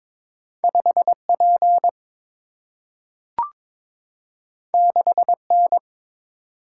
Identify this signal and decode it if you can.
Morse code, heard as a keyed tone, about 22 wpm, 700 Hz. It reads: 5 P 6 N
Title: Single-Letter Single-Number 22wpm